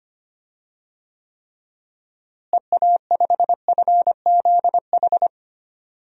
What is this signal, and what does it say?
Morse code, heard as a keyed tone, about 25 wpm, 700 Hz. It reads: EA5FZH
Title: DXCC-100 Call Signs - 6 Characters - Repeated 2x - 25wpm